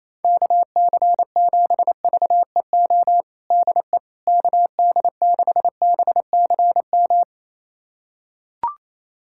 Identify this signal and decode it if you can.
Morse code, heard as a keyed tone, about 28 wpm, 700 Hz. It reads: KC7VEO DE KD6BCM
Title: USA Callsign DE Callsign - 6 Characters - 28wpm